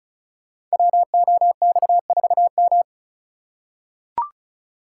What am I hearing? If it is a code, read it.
Morse code, heard as a keyed tone, about 34 wpm, 700 Hz. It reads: WOX4M
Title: Code Groups - 5 Letter:Numbers - 35wpm